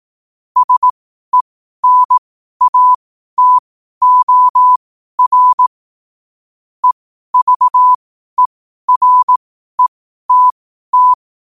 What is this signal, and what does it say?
Morse code, heard as a keyed tone, about 18 wpm, spaced as for 12 wpm, 1000 Hz. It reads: SENATOR EVERETT
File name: qotdfiles / 12wpm